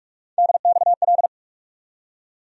Morse code, heard as a keyed tone, about 42 wpm, 700 Hz. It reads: DXL
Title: USA Call Sign Postfixes - 3 Letters 45wpm